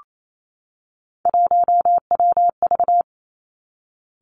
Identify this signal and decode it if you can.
Morse code, heard as a keyed tone, about 28 wpm, 700 Hz. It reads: 1WV
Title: Three Challenging Letter-Number Combinations 28wpm